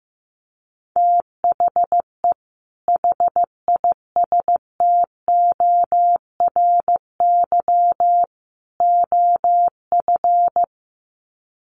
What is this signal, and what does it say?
Morse code, heard as a keyed tone, about 15 wpm, 700 Hz. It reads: THE HISTORY OF
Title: Sets of 3 Words 15wpm